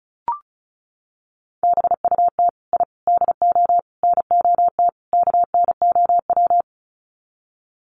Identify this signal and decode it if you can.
Morse code, heard as a keyed tone, about 34 wpm, 700 Hz. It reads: BUT I DO NOT KNOW
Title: Sets of 5 Words 35wpm